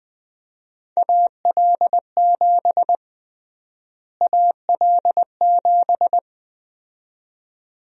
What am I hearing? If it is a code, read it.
Morse code, heard as a keyed tone, about 20 wpm, 700 Hz. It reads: AL7 AL7
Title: USA Call Sign Prefixes - Repeated 2x - 20wpm